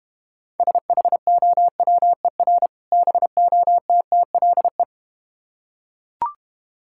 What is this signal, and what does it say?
Morse code, heard as a keyed tone, about 31 wpm, 700 Hz. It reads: SHOWER BOTTLE